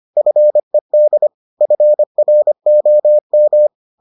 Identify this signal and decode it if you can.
Morse code, heard as a keyed tone, about 25 wpm, 600 Hz. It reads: FED FROM